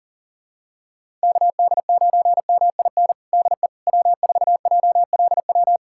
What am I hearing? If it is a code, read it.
Morse code, heard as a keyed tone, about 40 wpm, 700 Hz. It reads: KD9MIN DE W4JLW
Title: USA Callsign DE Callsign 40wpm